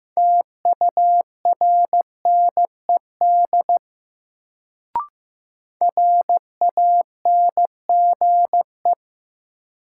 Morse code, heard as a keyed tone, about 15 wpm, 700 Hz. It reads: TURNED RANGE
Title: Sets of 4 Words - New Word Drill - 15wpm